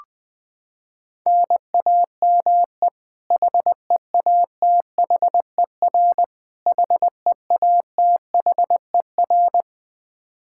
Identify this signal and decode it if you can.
Morse code, heard as a keyed tone, about 21 wpm, 700 Hz. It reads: NAME HEATHER HEATHER
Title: K1USN SST QSO Practice 20wpm